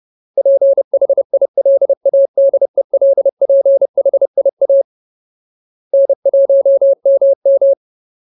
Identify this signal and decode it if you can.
Morse code, heard as a keyed tone, about 30 wpm, 550 Hz. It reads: PHILADELPHIA N1MM
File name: Ultimate Word List / 30WPM Ultimate